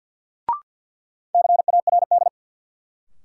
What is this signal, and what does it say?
Morse code, heard as a keyed tone, about 52 wpm, 700 Hz. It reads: CARD
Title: Sentences from Top 600 Words - New Word Drill - 50wpm